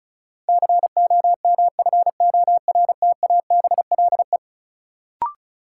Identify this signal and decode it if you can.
Morse code, heard as a keyed tone, about 36 wpm, 700 Hz. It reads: COMFORTABLE
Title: Sentences from Top 1000 Words - Review Part 7 - New Word Drill - 35wpm